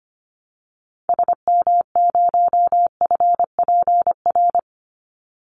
Morse code, heard as a keyed tone, about 25 wpm, 700 Hz. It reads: SM0FPR
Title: Call Signs Repeated 2x - 25wpm